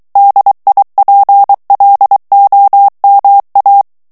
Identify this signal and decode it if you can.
Morse code, heard as a keyed tone, about 23 wpm, 800 Hz. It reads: DIPLOMA
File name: morse mp3 / ITALIEN texte clair 1 vitesse 22 mots minute